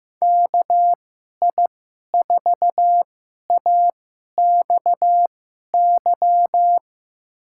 Morse code, heard as a keyed tone, about 15 wpm, spaced as for 11 wpm, 700 Hz. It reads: KI4AXY
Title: USA Call Signs - 2x Character Spacing - 15wpm